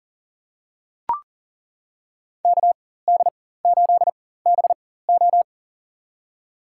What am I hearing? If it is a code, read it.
Morse code, heard as a keyed tone, about 40 wpm, spaced as for 19 wpm, 700 Hz. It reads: KD8BO
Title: USA Call Signs - 4x Character Spacing - 40wpm